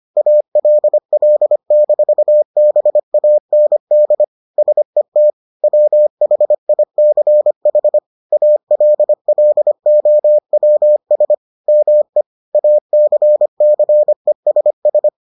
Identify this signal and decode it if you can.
Morse code, heard as a keyed tone, about 25 wpm, 600 Hz. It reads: ALL-BAND SET WHICH ALLOWS ME ACCESS